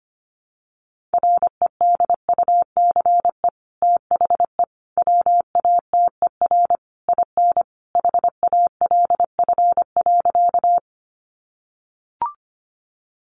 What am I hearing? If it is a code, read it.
Morse code, heard as a keyed tone, about 25 wpm, 700 Hz. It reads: REDUCE THE WATER IN HALF.
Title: Sentences from Top 1100 Words 25wpm